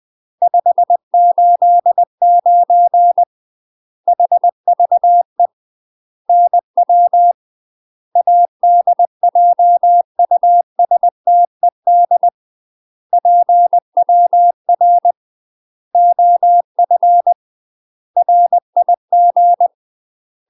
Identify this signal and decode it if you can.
Morse code, heard as a keyed tone, about 20 wpm, 700 Hz. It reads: 589 HVE NW ADJUSTED PWR OF RIG
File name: UK MASTER QSOs TRAINING / 20WPM / UK MASTER QSOs 0038